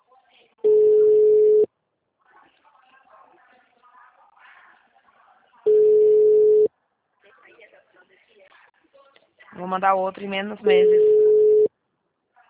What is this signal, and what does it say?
Ringback tone (Europe, 425 Hz), with speech.
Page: Dracena (Call-Center)